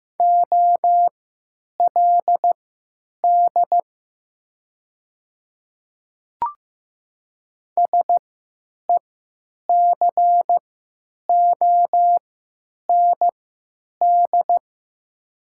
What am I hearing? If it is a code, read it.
Morse code, heard as a keyed tone, about 15 wpm, spaced as for 9 wpm, 700 Hz. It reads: OLD SECOND
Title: 3x Character Spacing - Top 300 Words - 15wpm